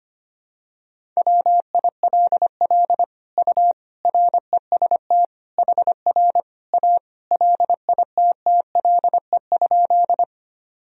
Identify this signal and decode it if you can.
Morse code, heard as a keyed tone, about 25 wpm, 700 Hz. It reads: WILL U REST HR A LITTLE?
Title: Sentences from Top 500 Words and Abbreviations 25wpm